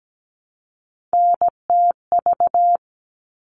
Morse code, heard as a keyed tone, about 17 wpm, 700 Hz. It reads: NTV
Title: All Cut Numbers - 3-Digit 17wpm